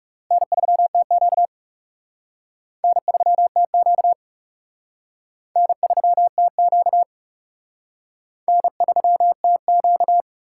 Morse code, heard as a keyed tone, about 38 wpm, 700 Hz. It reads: N3TQ N3TQ N3TQ N3TQ